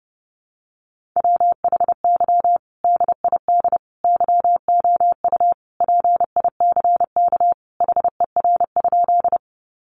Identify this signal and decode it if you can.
Morse code, heard as a keyed tone, about 30 wpm, 700 Hz. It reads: WHY DID YOU PICK HER?
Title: Sentences from Top 1000 Words 30wpm